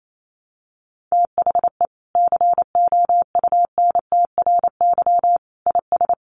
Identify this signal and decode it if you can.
Morse code, heard as a keyed tone, about 28 wpm, 700 Hz. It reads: THE COUNTRY IS
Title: Sentences from Top 1100 Words 28wpm